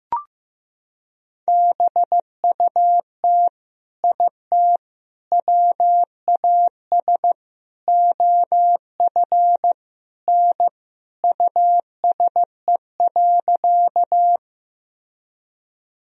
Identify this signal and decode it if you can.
Morse code, heard as a keyed tone, about 15 wpm, 700 Hz. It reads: BUT IT WAS OF N USE.
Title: Sentences from Top 100 Words and Abbreviations 15wpm